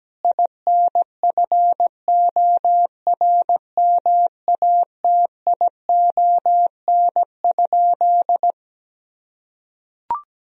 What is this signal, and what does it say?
Morse code, heard as a keyed tone, about 17 wpm, 700 Hz. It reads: INFORMATION?
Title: Sentences from Top 1000 Words - Review Part 3 - 17wpm